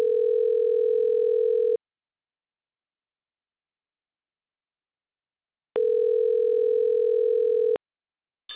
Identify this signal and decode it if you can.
Ringback tone (North America), 440 + 480 Hz.